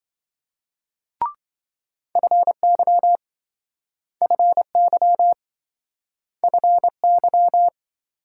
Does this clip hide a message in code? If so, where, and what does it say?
Yes: Morse code from 2.2 to 7.7 s, heard as a keyed tone, about 27 wpm, 700 Hz. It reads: FY FY FY